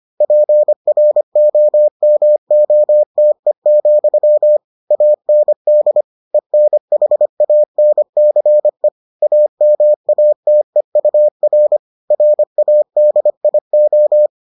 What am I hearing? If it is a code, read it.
Morse code, heard as a keyed tone, about 25 wpm, 600 Hz. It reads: PROMOTE, AND ENHANCE AMATEUR RADIO